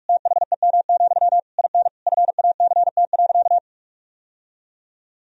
Morse code, heard as a keyed tone, about 46 wpm, 700 Hz. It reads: THEM, IN FACT.
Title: Sentences from Top 1000 Words - Review Part 3 - 45wpm